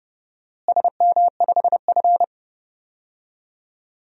Morse code, heard as a keyed tone, about 30 wpm, 700 Hz. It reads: SM5F
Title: DXCC-100 Call Signs - 4 Characters - Repeated 2x - 30wpm